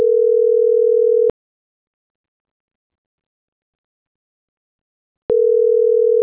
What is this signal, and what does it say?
Ringback tone (North America), 440 + 480 Hz.
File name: CENTRAL GENERAL / out daniel.valazco